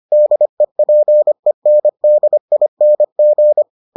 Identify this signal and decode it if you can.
Morse code, heard as a keyed tone, about 25 wpm, 600 Hz. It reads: DEPENDING